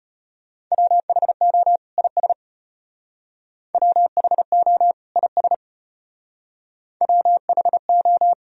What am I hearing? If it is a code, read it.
Morse code, heard as a keyed tone, about 32 wpm, 700 Hz. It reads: WHO IS WHO IS WHO